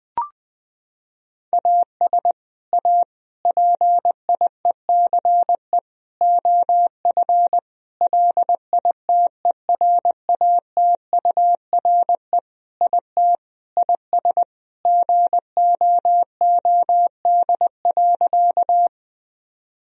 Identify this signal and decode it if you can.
Morse code, heard as a keyed tone, about 20 wpm, 700 Hz. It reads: AS A PIECE OF LITERATURE IT IS GOOD.
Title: Sentences from Top 1000 Words - Review Part 3 - 20wpm